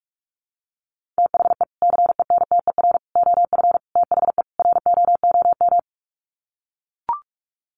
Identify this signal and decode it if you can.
Morse code, heard as a keyed tone, about 44 wpm, 700 Hz. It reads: THE CENTER OF THE ROOM
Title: Sets of 5 Words Encore 45wpm